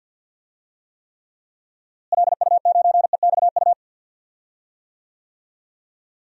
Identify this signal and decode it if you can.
Morse code, heard as a keyed tone, about 46 wpm, 700 Hz. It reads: LU9EXU
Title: DXCC-100 Call Signs - 6 Characters - Repeated 3x - 50wpm